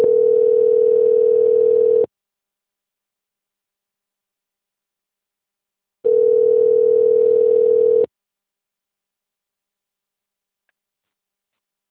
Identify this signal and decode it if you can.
Ringback tone (North America), 440 + 480 Hz.